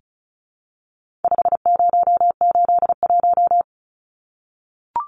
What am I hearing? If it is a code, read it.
Morse code, heard as a keyed tone, about 34 wpm, 700 Hz. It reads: 5081
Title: Numbers - 4-Digit 35wpm